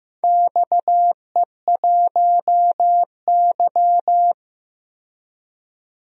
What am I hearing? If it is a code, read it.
Morse code, heard as a keyed tone, about 15 wpm, 700 Hz. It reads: XE1Y
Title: DXCC-100 Call Signs - 4 Characters - 15wpm